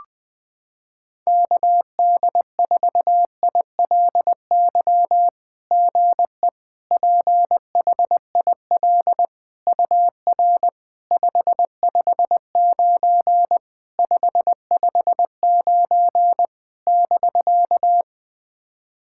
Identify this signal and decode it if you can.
Morse code, heard as a keyed tone, about 20 wpm, 700 Hz. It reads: KD4ILY GE PHIL UR 559 559 <BK>
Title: SOTA QSO Practice 20wpm